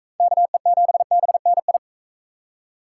Morse code, heard as a keyed tone, about 39 wpm, 700 Hz. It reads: KE7BNI